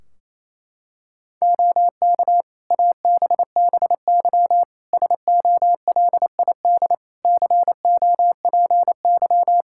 Morse code, heard as a keyed tone, about 28 wpm, 700 Hz. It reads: OK ABBY SOLID COPY
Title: Practice QSO - Exchange 3 - More Info - 28wpm